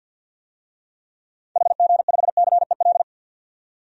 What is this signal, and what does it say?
Morse code, heard as a keyed tone, about 49 wpm, 700 Hz. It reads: HG5CEL
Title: DXCC-100 Call Signs - 6 Characters - 50wpm